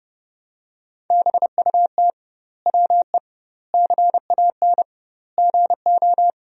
Morse code, heard as a keyed tone, about 30 wpm, 700 Hz. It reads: BUT WE CAN GO
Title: 2x Word-Spacing - Sentences from Top 100 Words - 30wpm